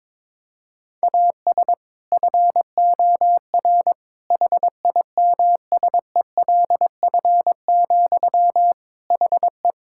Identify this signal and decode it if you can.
Morse code, heard as a keyed tone, about 22 wpm, 700 Hz. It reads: AS FOR HIMSELF, HE
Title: Sentences from Top 1600 Words 22wpm